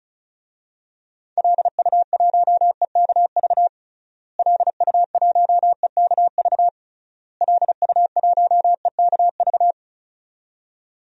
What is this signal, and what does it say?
Morse code, heard as a keyed tone, about 34 wpm, 700 Hz. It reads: LU1EKV LU1EKV LU1EKV
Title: DXCC-100 Call Signs - 6 Characters - Repeated 3x - 35wpm